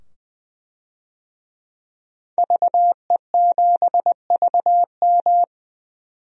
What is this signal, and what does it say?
Morse code, heard as a keyed tone, about 20 wpm, 700 Hz. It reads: VE7VM
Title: DXCC-100 Call Signs - 5 Characters - 20wpm